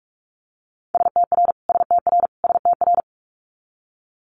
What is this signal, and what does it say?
Morse code, heard as a keyed tone, about 46 wpm, 700 Hz. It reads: STR STR STR
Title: Multi-letter Phonemes Repeated 3x 45wpm